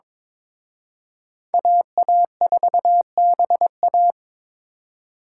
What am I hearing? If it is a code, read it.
Morse code, heard as a keyed tone, about 26 wpm, 700 Hz. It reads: AA4BA
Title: Speed Racing - USA Call Signs 22wpm